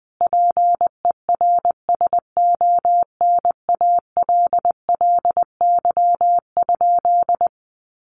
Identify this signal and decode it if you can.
Morse code, heard as a keyed tone, about 20 wpm, 700 Hz. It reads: PERSONALLY?
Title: Sentences from Top 1400 Words 20wpm